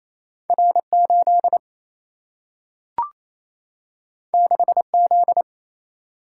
Morse code, heard as a keyed tone, about 28 wpm, 700 Hz. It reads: R 8 6 Z
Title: Single-Letter Single-Number 28wpm